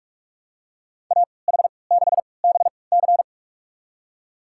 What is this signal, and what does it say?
Morse code, heard as a keyed tone, about 46 wpm, spaced as for 25 wpm, 700 Hz. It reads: AH6BC
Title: USA Call Signs - 5 Characters - 3x Character Spacing - 45wpm